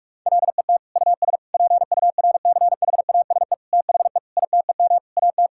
Morse code, heard as a keyed tone, about 46 wpm, 700 Hz. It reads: LET US PURCHASE THE ITEM AT
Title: Sentences from Top 1000 Words - Review Part 6 - 45wpm